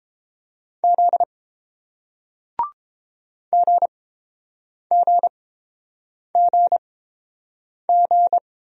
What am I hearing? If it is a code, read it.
Morse code, heard as a keyed tone, about 30 wpm, spaced as for 7 wpm, 700 Hz. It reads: Z GGGG